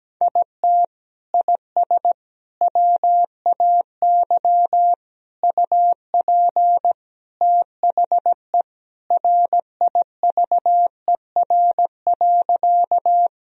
Morse code, heard as a keyed tone, about 17 wpm, 700 Hz. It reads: IT IS WAY UP THE RIVER.
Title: Sentences from Top 1000 Words - Review Part 1 - 17wpm